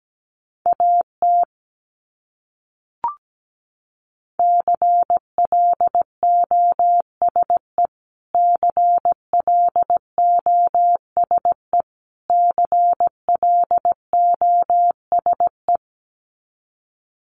Morse code, heard as a keyed tone, about 17 wpm, 700 Hz. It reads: AT CLOSE CLOSE CLOSE
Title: Top 500 Words Repeated 17wpm